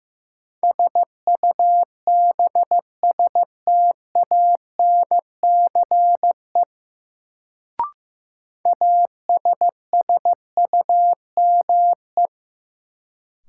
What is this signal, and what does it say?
Morse code, heard as a keyed tone, about 15 wpm, 700 Hz. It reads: SUBSTANCE ASSUME
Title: Sentences from Top 1500 Words - New Word Drill - 15wpm